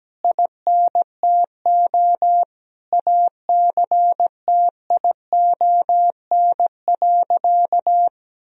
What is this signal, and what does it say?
Morse code, heard as a keyed tone, about 17 wpm, 700 Hz. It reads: INTO ACTION.
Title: Sentences from Top 1000 Words - Review Part 4 - 17wpm